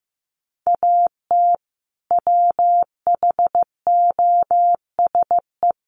Morse code, heard as a keyed tone, about 15 wpm, 700 Hz. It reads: AT WHOSE
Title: Sentences from Top 1200 Words 15wpm